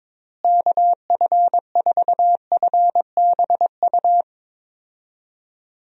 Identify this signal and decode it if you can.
Morse code, heard as a keyed tone, about 22 wpm, 700 Hz. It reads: KF4FBU
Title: USA Call Signs - 6 Characters - Repeated 2x - 22wpm